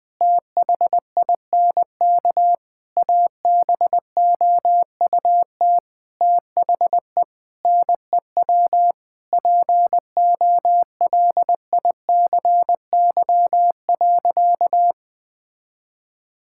Morse code, heard as a keyed tone, about 19 wpm, 700 Hz. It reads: THINK ABOUT THE NEW POLICY.
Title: Sentences with 8-Word Phrases - 20wpm